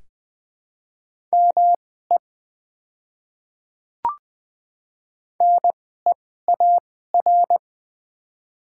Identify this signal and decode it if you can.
Morse code, heard as a keyed tone, about 21 wpm, spaced as for 15 wpm, 700 Hz. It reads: ME NEAR
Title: 2x Character Spacing - Top 300 Words - 20wpm